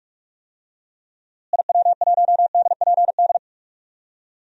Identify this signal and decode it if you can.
Morse code, heard as a keyed tone, about 46 wpm, 700 Hz. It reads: IW1DPD
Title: DXCC-100 Call Signs - 6 Characters - Repeated 2x - 45wpm